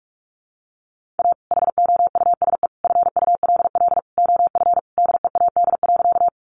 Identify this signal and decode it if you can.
Morse code, heard as a keyed tone, about 44 wpm, 700 Hz. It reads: A HOUSE FULL OF DEAD.
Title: Sentences from Top 1500 Words 45wpm